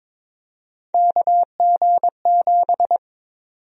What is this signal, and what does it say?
Morse code, heard as a keyed tone, about 21 wpm, 700 Hz. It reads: KG7
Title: USA Call Sign Prefixes - Repeated 2x - 22wpm